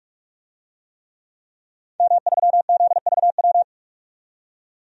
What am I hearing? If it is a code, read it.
Morse code, heard as a keyed tone, about 46 wpm, 700 Hz. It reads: M3ZVW
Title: DXCC-100 Call Signs - 5 Characters - 45wpm